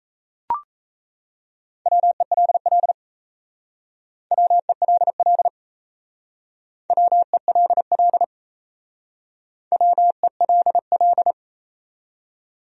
Morse code, heard as a keyed tone, about 35 wpm, 700 Hz. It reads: WELL WELL WELL WELL
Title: Speed-Racing Top 300 Words 28wpm